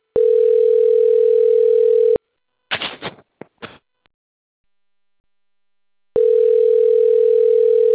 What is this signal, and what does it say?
Ringback tone (North America), 440 + 480 Hz.
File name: rg-1600-unknown- - - .25918